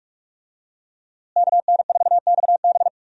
Morse code, heard as a keyed tone, about 46 wpm, 700 Hz. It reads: KN4XB
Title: USA Call Signs - 5 Characters - Repeated 3x - 45wpm